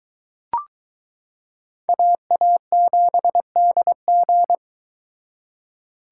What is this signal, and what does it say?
Morse code, heard as a keyed tone, about 23 wpm, 700 Hz. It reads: AA7DG